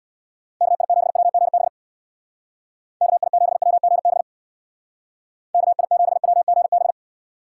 Morse code, heard as a keyed tone, about 58 wpm, 700 Hz. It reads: KI7WGB KI7WGB KI7WGB
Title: Speed Racing - USA Call Signs 50wpm